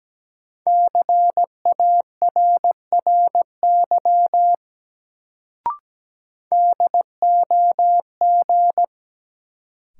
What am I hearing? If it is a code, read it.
Morse code, heard as a keyed tone, about 17 wpm, 700 Hz. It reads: CARRY DOG
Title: Sentences from Top 400 Words - New Word Drill - 17wpm